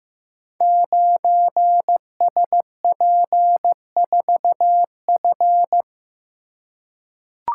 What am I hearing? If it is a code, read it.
Morse code, heard as a keyed tone, about 15 wpm, 700 Hz. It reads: 9SP4F
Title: Code Groups - 5 Letter:Numbers - 15wpm